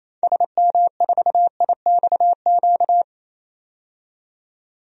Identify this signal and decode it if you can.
Morse code, heard as a keyed tone, about 28 wpm, 700 Hz. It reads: SM4IXQ
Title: DXCC-100 Call Signs - 6 Characters - 28wpm